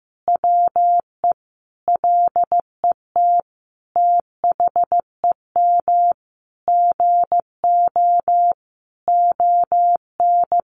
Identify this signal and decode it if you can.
Morse code, heard as a keyed tone, about 15 wpm, 700 Hz. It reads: WE LET THEM GO ON
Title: Sentences from Top 600 Words 15wpm